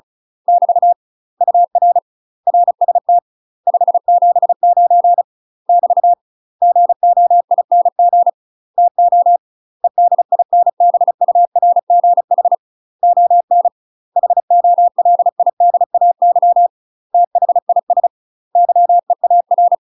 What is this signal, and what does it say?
Morse code, heard as a keyed tone, about 34 wpm, 700 Hz. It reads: <BT> UR RST 579 <BT> GOING TO EDINBURGH ON HOLIDAY THIS YEAR
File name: UK MASTER QSOs TRAINING/35WPM